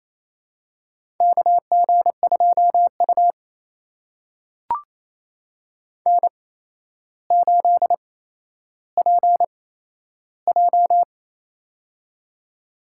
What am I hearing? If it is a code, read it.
Morse code, heard as a keyed tone, about 28 wpm, spaced as for 12 wpm, 700 Hz. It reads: KG2U N8PJ
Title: USA Call Signs - 4 Characters - 8x Character Spacing - 28wpm